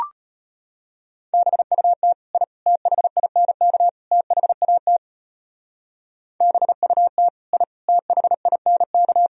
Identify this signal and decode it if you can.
Morse code, heard as a keyed tone, about 31 wpm, 700 Hz. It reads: BUT I THINK THAT BUT I THINK
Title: Speed-Racing - Sets of 4 Words 25wpm